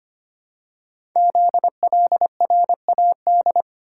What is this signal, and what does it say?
Morse code, heard as a keyed tone, about 25 wpm, 700 Hz. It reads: ZLRAD
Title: Code Groups - 5 Letter:Numbers - 25wpm